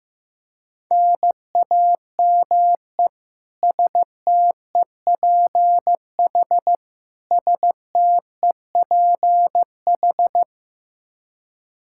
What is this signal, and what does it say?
Morse code, heard as a keyed tone, about 15 wpm, 700 Hz. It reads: NAME STEPH STEPH
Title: Practice QSO - Exchange 1 - Beginning - 15wpm